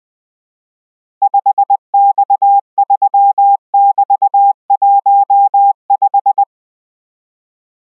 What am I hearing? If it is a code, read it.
Morse code, heard as a keyed tone, about 20 wpm, 800 Hz. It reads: 5X3<BT>15